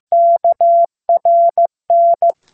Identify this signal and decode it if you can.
Morse code, heard as a keyed tone, about 15 wpm, 680 Hz. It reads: KRN